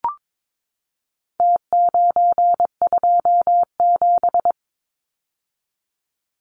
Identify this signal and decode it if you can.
Morse code, heard as a keyed tone, about 22 wpm, 700 Hz. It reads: T927
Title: Numbers - Contest Style - 4-Digit 22wpm